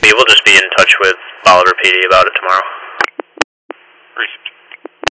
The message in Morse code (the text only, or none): none